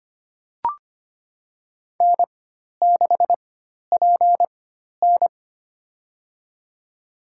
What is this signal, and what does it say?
Morse code, heard as a keyed tone, about 25 wpm, spaced as for 12 wpm, 700 Hz. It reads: N6PN
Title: USA Call Signs - 4 Characters - 4x Character Spacing - 25wpm